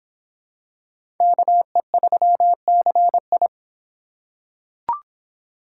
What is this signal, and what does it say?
Morse code, heard as a keyed tone, about 26 wpm, 700 Hz. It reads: KE3CI